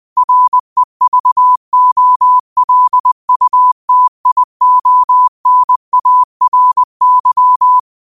Morse code, heard as a keyed tone, about 20 wpm, 1000 Hz. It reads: REVOLUTIONARY